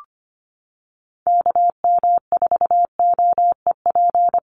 Morse code, heard as a keyed tone, about 25 wpm, 700 Hz. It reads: KM4OEP
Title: USA Call Signs Repeated 3x 25wpm